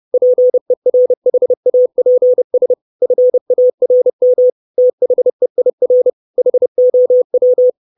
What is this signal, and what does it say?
Morse code, heard as a keyed tone, about 30 wpm, 500 Hz. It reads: PERHAPS FARM THEIR HOW
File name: 632 Words 30WPM